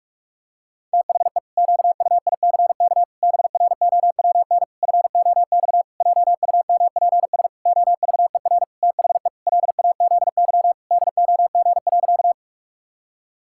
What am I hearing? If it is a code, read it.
Morse code, heard as a keyed tone, about 48 wpm, 700 Hz. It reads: THE QUICK BROWN FOX JUMPS OVER THE LAZY DOG.
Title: Pangram Challenge v2 - 45wpm